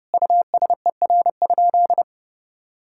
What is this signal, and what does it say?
Morse code, heard as a keyed tone, about 30 wpm, 700 Hz. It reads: USER?
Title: Sentences from Top 1000 Words - Review Part 5 - 30wpm